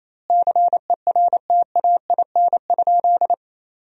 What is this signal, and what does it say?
Morse code, heard as a keyed tone, about 28 wpm, 700 Hz. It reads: CERTAIN?
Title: 2x Word-Spacing - Sentences from Top 500 Words - 28wpm